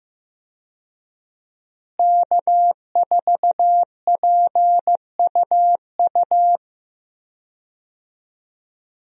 Morse code, heard as a keyed tone, about 15 wpm, 700 Hz. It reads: K4PUU
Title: USA Call Signs - 5 Characters - Repeated 3x - 15wpm